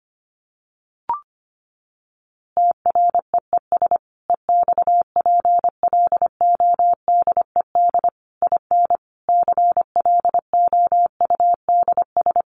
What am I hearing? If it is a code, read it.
Morse code, heard as a keyed tone, about 25 wpm, 700 Hz. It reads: TREES EXPLODED IN CLOUDS
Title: A Sound Of Thunder 25wpm